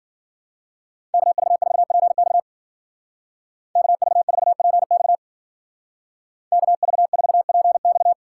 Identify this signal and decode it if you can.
Morse code, heard as a keyed tone, about 50 wpm, 700 Hz. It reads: KV4PX KV4PX KV4PX